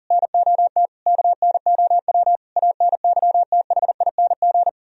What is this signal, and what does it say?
Morse code, heard as a keyed tone, about 52 wpm, 700 Hz. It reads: NOT KNOW ANYTHING